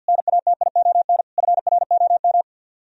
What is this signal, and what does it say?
Morse code, heard as a keyed tone, about 59 wpm, 700 Hz. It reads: NATION FROM
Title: Speed-Racing - Sets of 5 Words 50wpm